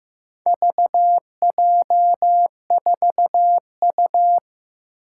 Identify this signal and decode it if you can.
Morse code, heard as a keyed tone, about 15 wpm, 700 Hz. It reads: VJ4U
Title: Four Challenging Letter-Number Combinations 15wpm